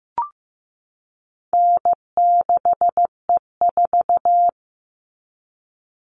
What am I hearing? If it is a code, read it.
Morse code, heard as a keyed tone, about 15 wpm, 700 Hz. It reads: N6E4
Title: All Cut Numbers - 4-Digit 15wpm